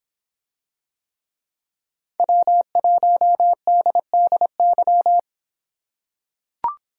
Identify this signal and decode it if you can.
Morse code, heard as a keyed tone, about 25 wpm, 700 Hz. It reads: W1DDY